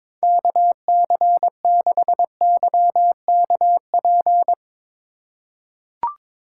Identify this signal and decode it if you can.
Morse code, heard as a keyed tone, about 22 wpm, 700 Hz. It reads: KC6YKP
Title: USA Call Signs - 6 Characters - 3x Character Spacing - 22wpm